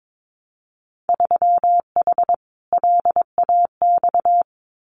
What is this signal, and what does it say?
Morse code, heard as a keyed tone, about 22 wpm, 700 Hz. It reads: 3H LAX
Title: Winter Field Day Exchanges 22wpm